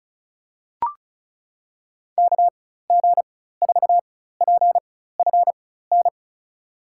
Morse code, heard as a keyed tone, about 36 wpm, spaced as for 16 wpm, 700 Hz. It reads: KG4PFN
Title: USA Call Signs - 6 Characters - 4x Character Spacing - 35wpm